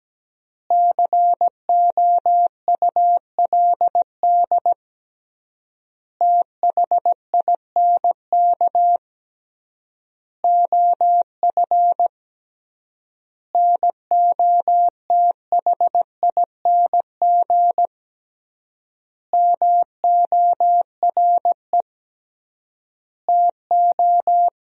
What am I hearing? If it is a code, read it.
Morse code, heard as a keyed tone, about 17 wpm, 700 Hz. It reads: COULD THINK OF NOTHING MORE TO
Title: 3x Word-Spacing - Sentences from Top 500 Words - 17wpm